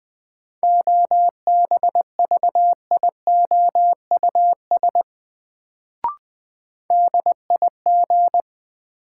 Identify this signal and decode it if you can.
Morse code, heard as a keyed tone, about 20 wpm, 700 Hz. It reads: OBVIOUS DIG
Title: Sentences from Top 1600 Words - New Word Drill Alt - 20wpm